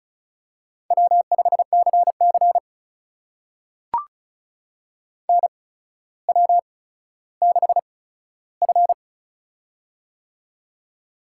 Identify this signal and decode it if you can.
Morse code, heard as a keyed tone, about 36 wpm, spaced as for 15 wpm, 700 Hz. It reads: W5CC NW6F
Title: USA Call Signs - 4 Characters - 8x Character Spacing - 35wpm